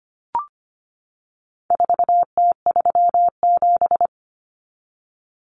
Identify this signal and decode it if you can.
Morse code, heard as a keyed tone, about 25 wpm, 700 Hz. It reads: 4T37
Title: Common Cut Numbers - 4-Digit 25wpm